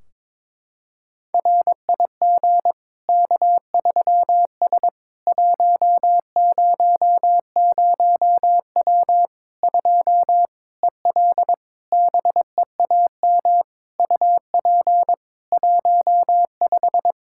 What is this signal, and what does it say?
Morse code, heard as a keyed tone, about 22 wpm, 700 Hz. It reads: RIG K3S 100W 2 EL BEAM UP 15
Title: Practice QSO - Exchange 2 - Info - 22wpm